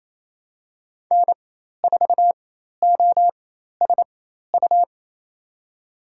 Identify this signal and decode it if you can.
Morse code, heard as a keyed tone, about 28 wpm, spaced as for 13 wpm, 700 Hz. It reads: N4OSU
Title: USA Call Signs - 5 Characters - 4x Character Spacing - 28wpm